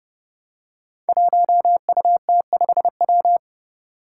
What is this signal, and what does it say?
Morse code, heard as a keyed tone, about 32 wpm, 700 Hz. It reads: 1UT5W
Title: Five Challenging Letter-Number Combinations 30wpm